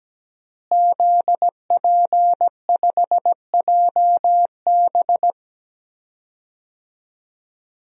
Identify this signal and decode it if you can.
Morse code, heard as a keyed tone, about 17 wpm, 700 Hz. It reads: ZP5JB
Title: DXCC-100 Call Signs - 5 Characters - Repeated 2x - 17wpm